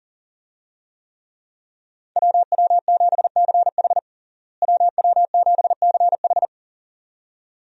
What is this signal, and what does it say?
Morse code, heard as a keyed tone, about 40 wpm, 700 Hz. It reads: WW7CH WW7CH
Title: USA Call Signs - 5 Characters - Repeated 2x - 40wpm